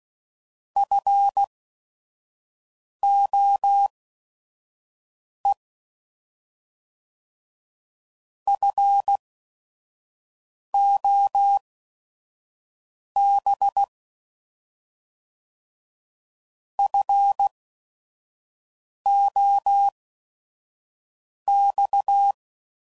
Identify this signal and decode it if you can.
Morse code, heard as a keyed tone, about 16 wpm, spaced as for 5 wpm, 800 Hz. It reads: FOE FOB FOX
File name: Prefixes / By First Letter / Fs